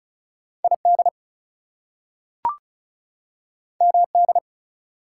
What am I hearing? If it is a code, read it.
Morse code, heard as a keyed tone, about 36 wpm, 700 Hz. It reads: ID MD
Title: US States and Canadian Provinces 35wpm